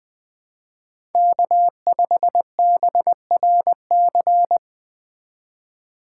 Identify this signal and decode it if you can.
Morse code, heard as a keyed tone, about 20 wpm, 700 Hz. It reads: K5BRC